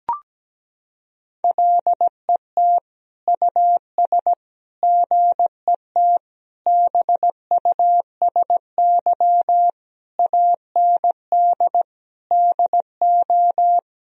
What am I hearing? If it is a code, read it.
Morse code, heard as a keyed tone, about 17 wpm, 700 Hz. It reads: LET US GET BUSY AND DO
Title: Sentences from Top 1000 Words - Review Part 3 - 17wpm